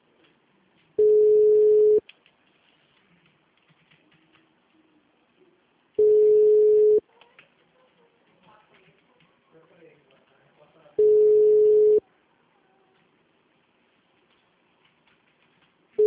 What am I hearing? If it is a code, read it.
Ringback tone (Europe, 425 Hz), with speech.
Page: CallCenter 2